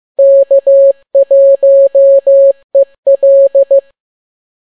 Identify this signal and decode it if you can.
Morse code, heard as a keyed tone, about 15 wpm, 550 Hz. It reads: K1EL